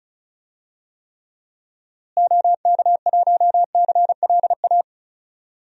Morse code, heard as a keyed tone, about 36 wpm, 700 Hz. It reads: OK1CLA
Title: DXCC-100 Call Signs - 6 Characters - Repeated 3x - 35wpm